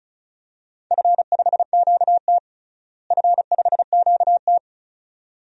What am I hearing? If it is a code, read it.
Morse code, heard as a keyed tone, about 36 wpm, 700 Hz. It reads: F5QT F5QT
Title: DXCC-100 Call Signs - 4 Characters - Repeated 2x - 35wpm